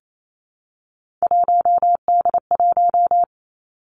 Morse code, heard as a keyed tone, about 28 wpm, 700 Hz. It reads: 1D1
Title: Three Challenging Letter-Number Combinations 28wpm